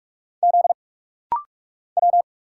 Morse code, heard as a keyed tone, about 45 wpm, 690 Hz. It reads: Z W
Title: Single Letters - Rapid-Fire 45wpm